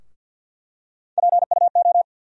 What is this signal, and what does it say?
Morse code, heard as a keyed tone, about 50 wpm, 700 Hz. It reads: PUO
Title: USA Call Sign Postfixes - 3 Letters 50wpm